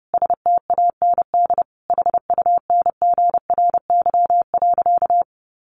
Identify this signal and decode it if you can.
Morse code, heard as a keyed tone, about 30 wpm, 700 Hz. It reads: STAND HUNGRY.
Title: Sentences from Top 1400 Words 30wpm